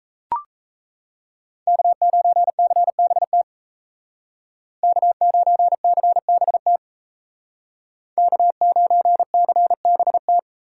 Morse code, heard as a keyed tone, about 33 wpm, 700 Hz. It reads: K9CBT K9CBT K9CBT
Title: Speed Racing - USA 1x3 Call Signs 28wpm